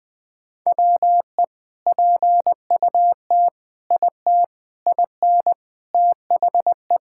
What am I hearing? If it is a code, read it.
Morse code, heard as a keyed tone, about 20 wpm, 700 Hz. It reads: WE PUT IT IN THE
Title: Sentences from Top 1000 Words - Review Part 5 - 20wpm